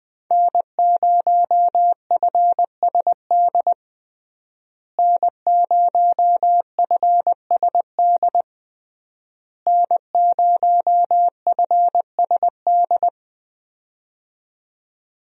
Morse code, heard as a keyed tone, about 21 wpm, 700 Hz. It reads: N0FSD N0FSD N0FSD
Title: USA Call Signs - 5 Characters - Repeated 3x - 20wpm